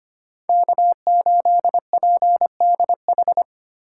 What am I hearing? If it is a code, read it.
Morse code, heard as a keyed tone, about 25 wpm, 700 Hz. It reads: K8PDH
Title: USA Call Signs - 3x Character Spacing - 25wpm